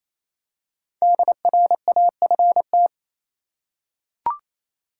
Morse code, heard as a keyed tone, about 28 wpm, 700 Hz. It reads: DRAFT
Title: Five Letter Words - 28wpm